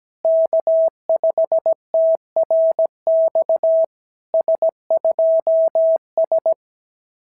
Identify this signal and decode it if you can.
Morse code, heard as a keyed tone, about 17 wpm, 650 Hz. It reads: K5TRX S2S
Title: SOTA QSO Practice 17wpm